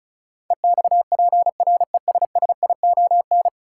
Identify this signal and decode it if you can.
Morse code, heard as a keyed tone, about 34 wpm, 700 Hz. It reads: EXPRESSION
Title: Sentences from Top 1200 Words - New Word Drill - 35wpm